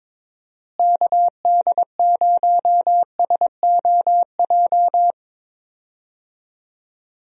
Morse code, heard as a keyed tone, about 22 wpm, 700 Hz. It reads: KD0SOJ
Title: USA Call Signs - 6 Characters - Repeated 2x - 22wpm